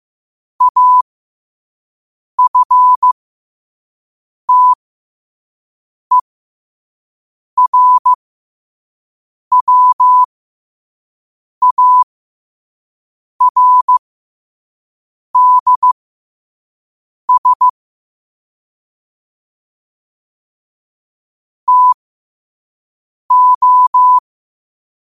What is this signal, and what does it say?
Morse code, heard as a keyed tone, about 15 wpm, spaced as for 5 wpm, 1000 Hz. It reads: AFTERWARDS TO
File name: 5wpm